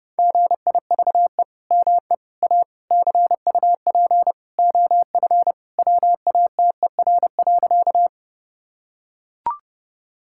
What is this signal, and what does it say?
Morse code, heard as a keyed tone, about 30 wpm, 700 Hz. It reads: GIVE ME A CUP OF WATER.
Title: Sentences from Top 1600 Words 30wpm